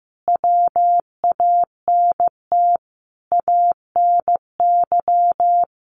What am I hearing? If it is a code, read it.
Morse code, heard as a keyed tone, about 14 wpm, 700 Hz. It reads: WANT ANY
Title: Sentences from Top 1400 Words 15wpm